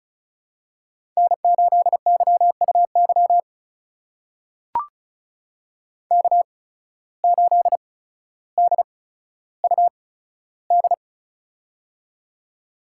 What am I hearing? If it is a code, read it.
Morse code, heard as a keyed tone, about 34 wpm, spaced as for 15 wpm, 700 Hz. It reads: N8YUY K8DUD
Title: USA Call Signs - 5 Characters - 8x Character Spacing - 35wpm